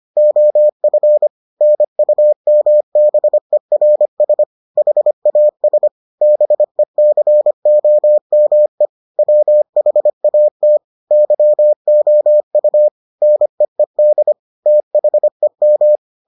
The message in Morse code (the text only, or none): OF NUMBERS HAS BECOME WHAT YOU NEED THEM